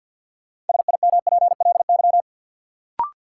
Morse code, heard as a keyed tone, about 53 wpm, 700 Hz. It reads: SIMPLY
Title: Sentences from Top 600 Words - New Word Drill - 50wpm